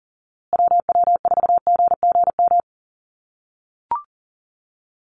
Morse code, heard as a keyed tone, about 40 wpm, 700 Hz. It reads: WW4GGM
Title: USA Call Signs Repeated 2x 40wpm